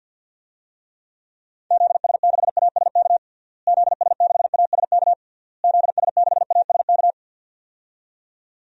Morse code, heard as a keyed tone, about 47 wpm, 700 Hz. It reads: ZS6ASK ZS6ASK ZS6ASK
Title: DXCC-100 Call Signs - 6 Characters - Repeated 3x - 50wpm